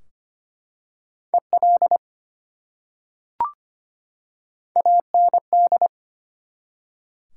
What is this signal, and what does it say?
Morse code, heard as a keyed tone, about 25 wpm, 700 Hz. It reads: EL AND
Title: Top 180 QSO Elements 25wpm